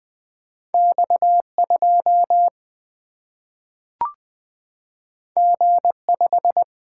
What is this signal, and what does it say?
Morse code, heard as a keyed tone, about 20 wpm, 700 Hz. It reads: X 2 G 5
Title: Single-Letter Single-Number 20wpm